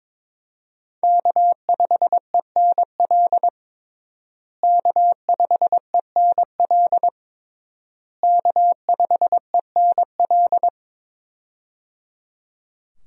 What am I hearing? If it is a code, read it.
Morse code, heard as a keyed tone, about 22 wpm, 700 Hz. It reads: K5ENL K5ENL K5ENL
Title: USA Call Signs - 5 Characters - Repeated 3x - 22wpm